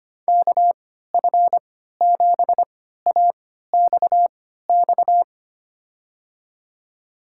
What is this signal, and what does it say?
Morse code, heard as a keyed tone, about 25 wpm, spaced as for 14 wpm, 700 Hz. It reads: KF7AXX
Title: USA Call Signs - 6 Characters - 3x Character Spacing - 25wpm